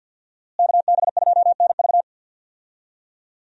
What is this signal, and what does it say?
Morse code, heard as a keyed tone, about 52 wpm, 700 Hz. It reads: KB2NV
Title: USA Call Signs - 5 Characters - Repeated 2x - 50wpm